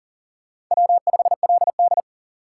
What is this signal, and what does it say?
Morse code, heard as a keyed tone, about 40 wpm, 700 Hz. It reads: W5LD
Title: USA Call Signs - 4 Characters - 4x Character Spacing - 40wpm